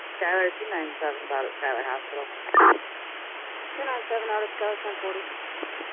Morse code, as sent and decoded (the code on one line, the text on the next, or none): none